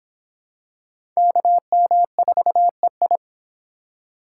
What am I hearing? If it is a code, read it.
Morse code, heard as a keyed tone, about 24 wpm, 700 Hz. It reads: KM4EI